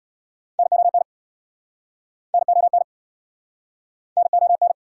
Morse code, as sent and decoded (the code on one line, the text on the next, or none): -. --- -. / -. --- -. / -. --- -.
NON NON NON